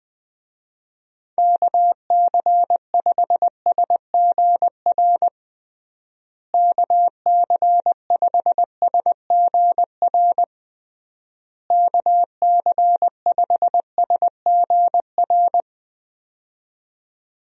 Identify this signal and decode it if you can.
Morse code, heard as a keyed tone, about 20 wpm, 700 Hz. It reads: KC5SGR KC5SGR KC5SGR
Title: USA Call Signs - 6 Characters - Repeated 3x - 20wpm